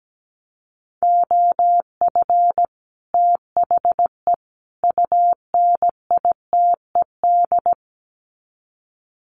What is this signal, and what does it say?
Morse code, heard as a keyed tone, about 17 wpm, 700 Hz. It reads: OF THE UNITED
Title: Sets of 3 Words 17wpm